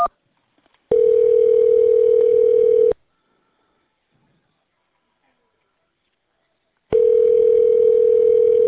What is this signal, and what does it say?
Ringback tone (North America), 440 + 480 Hz.